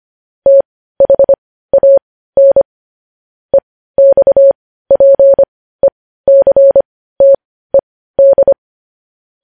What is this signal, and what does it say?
Morse code, heard as a keyed tone, about 25 wpm, spaced as for 15 wpm, 550 Hz. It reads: THAN EXPECTED
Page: news headlines in morse code at 15 wpm